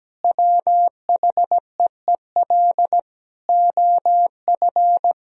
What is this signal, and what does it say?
Morse code, heard as a keyed tone, about 17 wpm, 700 Hz. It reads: WHEEL OF
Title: Sentences from Top 1000 Words - Review Part 3 - 17wpm